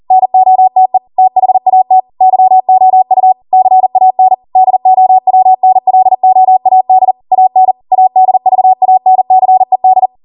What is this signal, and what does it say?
Morse code, heard as a keyed tone, about 40 wpm, 750 Hz. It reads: NOTE THAT YOU CAN DOWNLOAD AN ADVANCED
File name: Morse / 40WPM